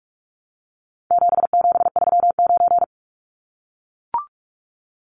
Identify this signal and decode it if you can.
Morse code, heard as a keyed tone, about 44 wpm, 700 Hz. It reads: 7739
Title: Numbers - 4-Digit 45wpm